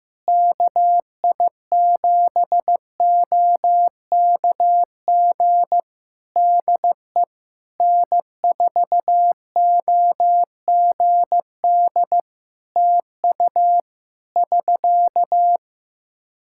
Morse code, heard as a keyed tone, about 16 wpm, 700 Hz. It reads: KI7OKG DE N4OGD TU <SK>
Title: K1USN SST QSO Practice 15wpm